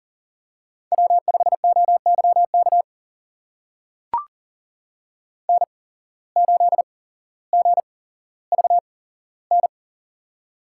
Morse code, heard as a keyed tone, about 42 wpm, spaced as for 17 wpm, 700 Hz. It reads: W5OYK N8GVN
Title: USA Call Signs - 5 Characters - 8x Character Spacing - 40wpm